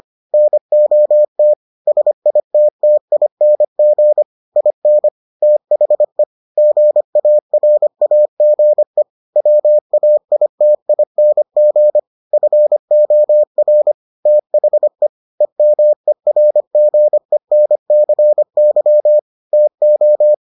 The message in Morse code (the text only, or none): NOT SITTING IN THE GARAGE WAITING FOR THE EMERGENCY TO